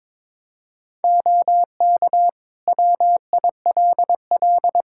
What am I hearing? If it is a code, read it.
Morse code, heard as a keyed tone, about 22 wpm, 700 Hz. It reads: OK WILL
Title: Practice QSO - Exchange 2 - Info - 22wpm